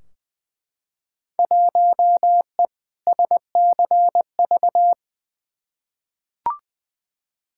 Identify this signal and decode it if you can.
Morse code, heard as a keyed tone, about 21 wpm, 700 Hz. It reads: 1E SCV
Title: Field Day Exchanges - 20wpm